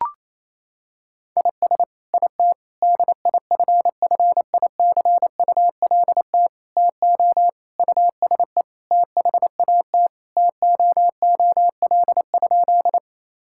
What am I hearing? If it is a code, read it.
Morse code, heard as a keyed tone, about 28 wpm, 700 Hz. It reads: IS IT DIFFICULT TO USE THAT TOOL?
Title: Sentences from Top 1000 Words - Review Part 6 - 28wpm